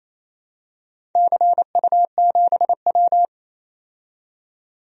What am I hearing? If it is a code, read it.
Morse code, heard as a keyed tone, about 28 wpm, 700 Hz. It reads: CU7W
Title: DXCC-100 Call Signs - 4 Characters - 28wpm